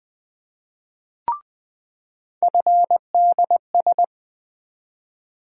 Morse code, heard as a keyed tone, about 20 wpm, 700 Hz. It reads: FDS
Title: USA Call Sign Postfixes - 3 Letters 20wpm